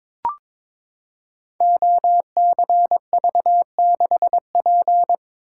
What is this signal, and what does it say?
Morse code, heard as a keyed tone, about 22 wpm, 700 Hz. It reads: OCV6P
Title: Code Groups - 5 Letter:Numbers - 22wpm